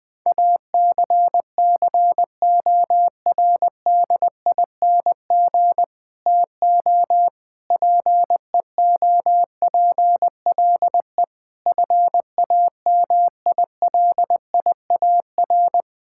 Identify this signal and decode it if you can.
Morse code, heard as a keyed tone, about 21 wpm, 700 Hz. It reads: ACCORDING TO PEOPLE FAMILIAR
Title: Sentences with 7-Word Phrases - Spoken Beginning - 20wpm